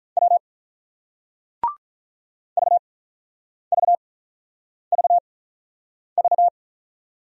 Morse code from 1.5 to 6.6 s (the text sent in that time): VVVV